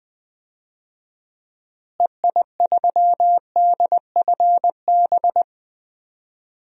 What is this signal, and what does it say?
Morse code, heard as a keyed tone, about 20 wpm, 700 Hz. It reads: EI3DFB
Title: DXCC-100 Call Signs - 6 Characters - Repeated 3x - 20wpm